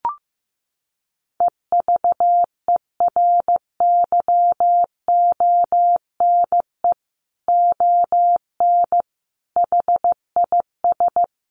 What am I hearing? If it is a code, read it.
Morse code, heard as a keyed tone, about 15 wpm, 700 Hz. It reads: EVERYONE ON HIS
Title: A Sound Of Thunder 15wpm